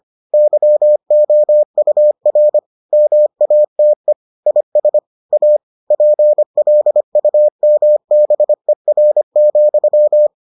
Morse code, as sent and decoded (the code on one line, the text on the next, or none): -.-- --- ..- .-. / -- .- - . / .. ... / .- / .--. .-.. ..- -- -... .-. --..--
YOUR MATE IS A PLUMBER,